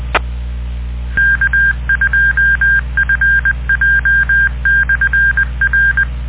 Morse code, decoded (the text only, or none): K2FJ/R